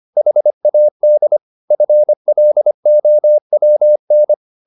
Morse code, heard as a keyed tone, about 25 wpm, 600 Hz. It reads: HAD FLOWN
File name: the old man and the sea part 1 25wpm